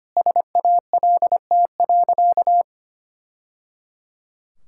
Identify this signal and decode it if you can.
Morse code, heard as a keyed tone, about 25 wpm, 700 Hz. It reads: SALT.
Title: Sentences from Top 1000 Words - Review Part 5 - 25wpm